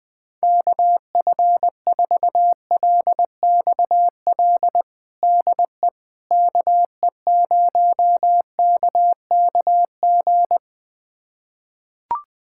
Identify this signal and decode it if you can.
Morse code, heard as a keyed tone, about 21 wpm, 700 Hz. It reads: KF4LXL DE KE0KKG
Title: USA Callsign DE Callsign - 6 Characters - 20wpm